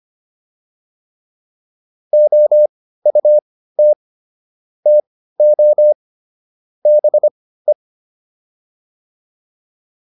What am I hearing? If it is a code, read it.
Morse code, heard as a keyed tone, about 25 wpm, spaced as for 15 wpm, 600 Hz. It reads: OUT TO BE